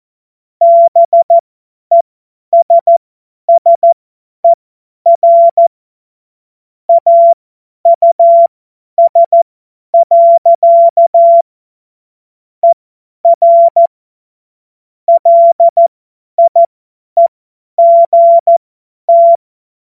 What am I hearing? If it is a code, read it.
Morse code, heard as a keyed tone, about 14 wpm, spaced as for 10 wpm, 680 Hz. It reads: BESSER AUS. ER LIEGT
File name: Morsekurs / Witze / Witz 11. Januar 2022, 50 F70